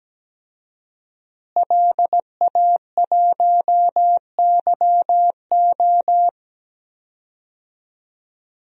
Morse code, heard as a keyed tone, about 17 wpm, 700 Hz. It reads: LA1YO
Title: DXCC-100 Call Signs - 5 Characters - Repeated 3x - 17wpm